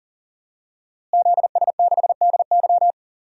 Morse code, heard as a keyed tone, about 40 wpm, 700 Hz. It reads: ZS6DY
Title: DXCC-100 Call Signs - 5 Characters - Repeated 2x - 40wpm